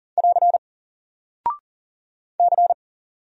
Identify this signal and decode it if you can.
Morse code, heard as a keyed tone, about 39 wpm, 700 Hz. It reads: <AR> C
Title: Single Character (Letter:Number:Punctuation:Prosign) 40wpm